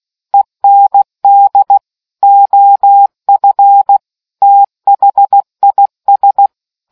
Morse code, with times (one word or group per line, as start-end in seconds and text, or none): 0.3-1.8 END
2.2-4.0 OF
4.4-6.5 THIS